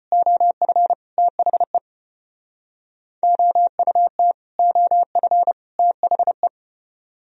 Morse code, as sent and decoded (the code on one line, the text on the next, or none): --- ..-. / - .... / --- ..- - / --- ..-. / - .... .
OF THE OUT OF THE